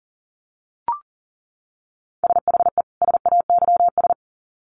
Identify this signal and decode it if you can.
Morse code, heard as a keyed tone, about 40 wpm, 700 Hz. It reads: SHE SAYS
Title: Sets of 2 Words 40wpm